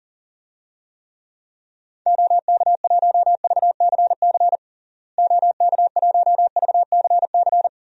Morse code, heard as a keyed tone, about 40 wpm, 700 Hz. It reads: OK1VCC OK1VCC
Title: DXCC-100 Call Signs - 6 Characters - Repeated 2x - 40wpm